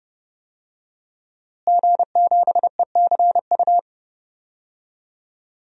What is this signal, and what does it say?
Morse code, heard as a keyed tone, about 28 wpm, 700 Hz. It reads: G7ECU
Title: DXCC-100 Call Signs - 5 Characters - Repeated 3x - 30wpm